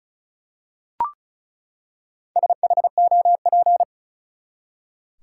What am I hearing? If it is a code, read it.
Morse code, heard as a keyed tone, about 34 wpm, 700 Hz. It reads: SHOP
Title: Four Letter Words - 35wpm